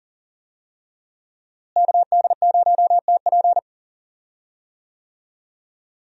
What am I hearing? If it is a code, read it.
Morse code, heard as a keyed tone, about 39 wpm, 700 Hz. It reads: KD0TP
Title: USA Call Signs - 5 Characters - Repeated 3x - 40wpm